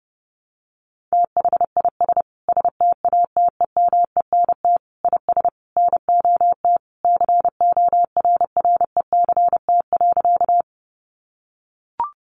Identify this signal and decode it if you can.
Morse code, heard as a keyed tone, about 30 wpm, 700 Hz. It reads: THIS STATEMENT IS NOT CORRECT.
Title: Sentences from Top 1200 Words 30wpm